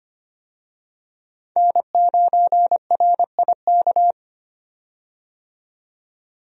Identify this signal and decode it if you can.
Morse code, heard as a keyed tone, about 25 wpm, 700 Hz. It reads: N9RIK
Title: USA Call Signs - 5 Characters - 25wpm